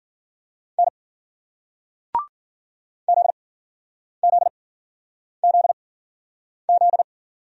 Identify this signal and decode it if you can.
Morse code, heard as a keyed tone, about 53 wpm, spaced as for 9 wpm, 700 Hz. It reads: N ZZZZ